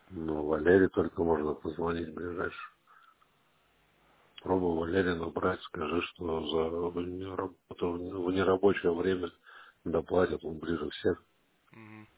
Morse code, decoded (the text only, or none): none